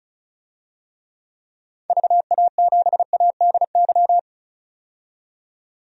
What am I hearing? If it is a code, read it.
Morse code, heard as a keyed tone, about 35 wpm, 700 Hz. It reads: VA7ADY
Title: DXCC-100 Call Signs - 6 Characters - 35wpm